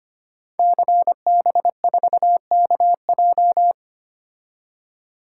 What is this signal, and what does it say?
Morse code, heard as a keyed tone, about 25 wpm, 700 Hz. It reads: CB4KJ
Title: Code Groups - 5 Letter:Numbers - 25wpm